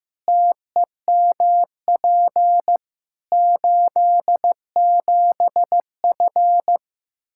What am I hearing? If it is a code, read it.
Morse code, heard as a keyed tone, about 15 wpm, 700 Hz. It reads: TEMP 87F
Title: Practice QSO - Exchange 2 - Info - 15wpm